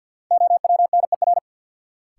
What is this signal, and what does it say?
Morse code, heard as a keyed tone, about 49 wpm, 700 Hz. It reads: OWNER
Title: Sentences from Top 1600 Words - New Word Drill - 50wpm